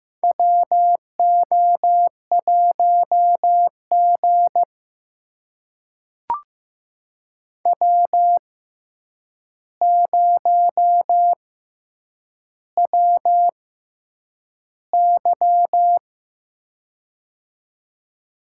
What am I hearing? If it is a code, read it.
Morse code, heard as a keyed tone, about 15 wpm, spaced as for 8 wpm, 700 Hz. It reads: WO1G W0WY
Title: USA Call Signs - 4 Characters - 6x Character Spacing - 15wpm